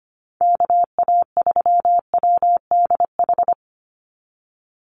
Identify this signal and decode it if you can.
Morse code, heard as a keyed tone, about 25 wpm, 700 Hz. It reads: KA3WDH
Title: USA Call Signs 25wpm